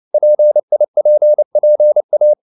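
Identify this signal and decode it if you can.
Morse code, heard as a keyed tone, about 29 wpm, 600 Hz. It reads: PIPPA
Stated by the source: Morse code at 29 wpm, sending Ken Follett's novel Never in English, from CW med Gnister.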